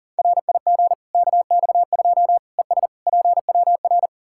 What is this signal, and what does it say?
Morse code, heard as a keyed tone, about 37 wpm, 700 Hz. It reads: RIG KX2 ES PWR
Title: K1USN SST QSO Practice 40wpm